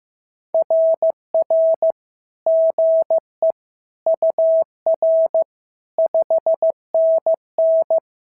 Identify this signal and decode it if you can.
Morse code, heard as a keyed tone, about 15 wpm, 650 Hz. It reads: RR GE UR 5NN
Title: SOTA QSO Practice 15wpm